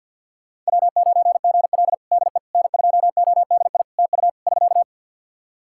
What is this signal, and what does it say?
Morse code, heard as a keyed tone, about 52 wpm, 700 Hz. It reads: W9GL DE N2ODI TU <SK>
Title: Practice QSO - Exchange 4 - Ending - 50wpm